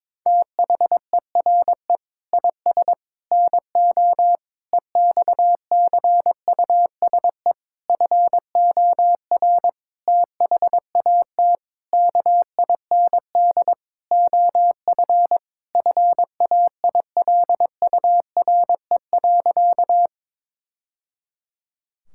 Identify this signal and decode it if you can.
Morse code, heard as a keyed tone, about 22 wpm, 700 Hz. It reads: THERE IS NO EXCUSE FOR THAT KIND OF FAILURE.
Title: Sentences from Top 1000 Words - Review Part 6 - 22wpm